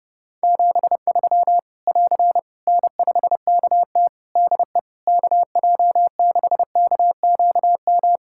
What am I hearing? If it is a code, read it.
Morse code, heard as a keyed tone, about 32 wpm, 700 Hz. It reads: 73 <AR> N5KT DE KJ6KQM
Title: Practice QSO - Exchange 4 - Ending - 30wpm